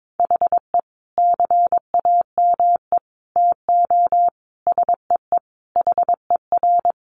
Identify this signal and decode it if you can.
Morse code, heard as a keyed tone, about 22 wpm, 700 Hz. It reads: HE CAME TO SEE HER
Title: Sentences from Top 1300 Words 22wpm